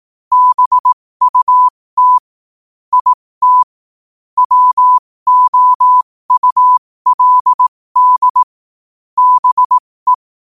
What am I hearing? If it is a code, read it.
Morse code, heard as a keyed tone, about 18 wpm, spaced as for 15 wpm, 1000 Hz. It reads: BUT IT WOULD BE
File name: qotdfiles / 15wpm